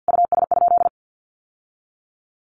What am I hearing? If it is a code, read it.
Morse code, heard as a keyed tone, about 50 wpm, 700 Hz. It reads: US?
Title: Sentences from Top 1400 Words 50wpm